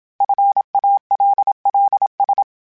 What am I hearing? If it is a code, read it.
Morse code, heard as a keyed tone, about 27 wpm, 800 Hz. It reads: FALLS
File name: fromSMs / FromSMs mp3s / Cities@25